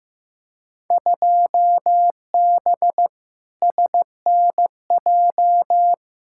Morse code, heard as a keyed tone, about 15 wpm, 700 Hz. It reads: 2B SNJ
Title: Field Day Exchanges Encore - 15wpm